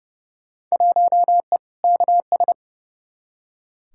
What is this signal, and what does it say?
Morse code, heard as a keyed tone, about 30 wpm, 700 Hz. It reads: 1E KS
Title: Field Day Exchanges - 30wpm